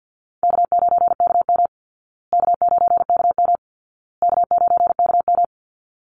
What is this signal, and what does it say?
Morse code, heard as a keyed tone, about 50 wpm, 700 Hz. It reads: K9KM K9KM K9KM
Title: Call Signs Repeated 2x - 50wpm